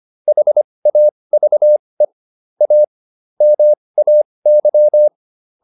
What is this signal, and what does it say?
Morse code, heard as a keyed tone, about 25 wpm, spaced as for 20 wpm, 600 Hz. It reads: HAVE A MAY